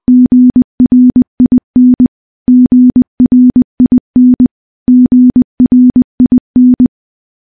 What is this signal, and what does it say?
Morse code, heard as a keyed tone, about 20 wpm, 250 Hz. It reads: GRIN GRIN GRIN
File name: morse / morse-training-20WPM